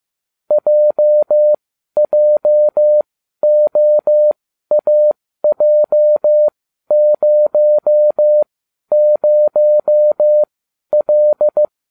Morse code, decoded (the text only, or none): JJOAJ00L